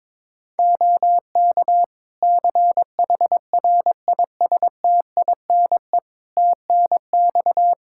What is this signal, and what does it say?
Morse code, heard as a keyed tone, about 22 wpm, 700 Hz. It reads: OK CHRISTINE TNX
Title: Practice QSO - Exchange 4 - Ending - 22wpm